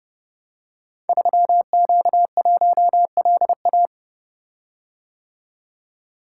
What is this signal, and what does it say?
Morse code, heard as a keyed tone, about 30 wpm, 700 Hz. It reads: 3Q1LA
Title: Code Groups - 5 Letter:Numbers - 30wpm